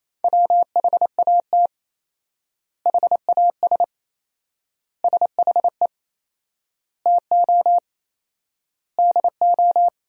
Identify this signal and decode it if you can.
Morse code, heard as a keyed tone, about 28 wpm, 700 Hz. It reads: WHAT HAS SHE TO DO
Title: 4x Word-Spacing - Sentences from Top 100 Words - 28wpm